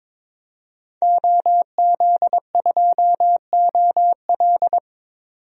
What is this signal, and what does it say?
Morse code, heard as a keyed tone, about 22 wpm, 700 Hz. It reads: OZ2OL
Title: DXCC-100 Call Signs - 5 Characters - 22wpm